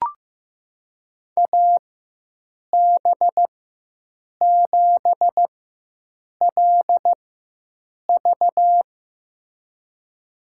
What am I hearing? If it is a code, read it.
Morse code, heard as a keyed tone, about 15 wpm, spaced as for 7 wpm, 700 Hz. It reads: AB7LV
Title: USA Call Signs - 4x Character Spacing - 15wpm